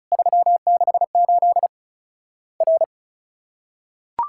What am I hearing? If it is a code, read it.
Morse code, heard as a keyed tone, about 34 wpm, 700 Hz. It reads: 368 R
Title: Worked All Europe DX Contest - QTC Exchange - 35wpm